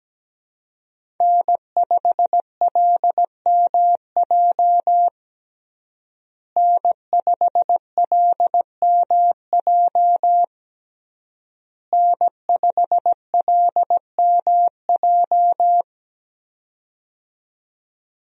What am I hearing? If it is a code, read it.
Morse code, heard as a keyed tone, about 17 wpm, 700 Hz. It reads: N5LMJ N5LMJ N5LMJ
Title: USA Call Signs - 5 Characters - Repeated 3x - 17wpm